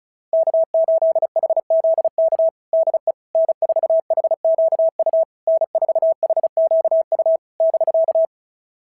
Morse code, heard as a keyed tone, about 33 wpm, 660 Hz. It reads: K8HZK DE N4HQU N4HQU <BK>
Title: SOTA QSO Practice 35wpm